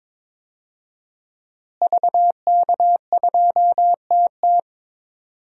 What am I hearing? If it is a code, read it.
Morse code, heard as a keyed tone, about 22 wpm, 700 Hz. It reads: VK2TT
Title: DXCC-100 Call Signs - 5 Characters - Repeated 3x - 22wpm